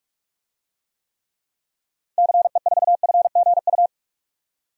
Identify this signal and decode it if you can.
Morse code, heard as a keyed tone, about 46 wpm, 700 Hz. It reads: CE4FGU
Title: DXCC-100 Call Signs - 6 Characters - 45wpm